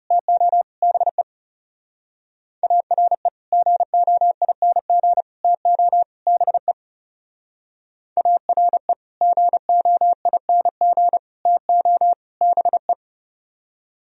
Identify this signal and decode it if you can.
Morse code, heard as a keyed tone, about 36 wpm, 700 Hz. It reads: TO BE ARE GOING TO BE ARE GOING TO BE
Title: Speed-Racing - Sets of 4 Words 30wpm